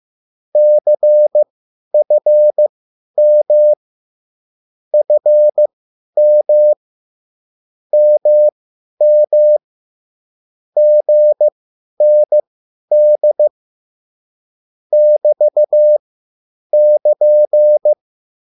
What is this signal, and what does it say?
Morse code, heard as a keyed tone, about 15 wpm, spaced as for 10 wpm, 600 Hz. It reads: CFM FM MM GND <BT><KN>